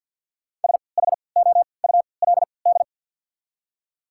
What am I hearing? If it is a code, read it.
Morse code, heard as a keyed tone, about 53 wpm, spaced as for 29 wpm, 690 Hz. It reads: SHOULD
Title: 3x Character Spacing - Top 300 Words - 50wpm